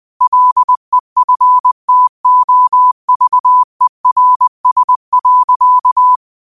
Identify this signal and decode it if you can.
Morse code, heard as a keyed tone, about 20 wpm, 1000 Hz. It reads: LEFTOVERS.